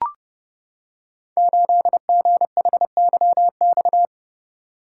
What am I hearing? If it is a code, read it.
Morse code, heard as a keyed tone, about 30 wpm, 700 Hz. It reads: 8GHYX
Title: Code Groups - 5 Letter:Numbers - 30wpm